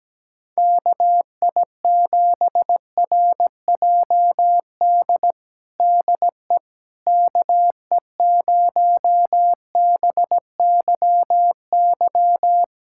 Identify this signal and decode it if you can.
Morse code, heard as a keyed tone, about 17 wpm, 700 Hz. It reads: KI7RJD DE KE0BYY
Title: USA Callsign DE Callsign - 6 Characters - 17wpm